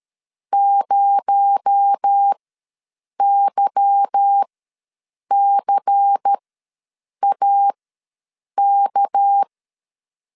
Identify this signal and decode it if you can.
Morse code, heard as a keyed tone, about 13 wpm, spaced as for 7 wpm, 800 Hz. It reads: 0YCAK
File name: LEZIONI / 8 YJ45 / 8 Y ws 12wpm 3409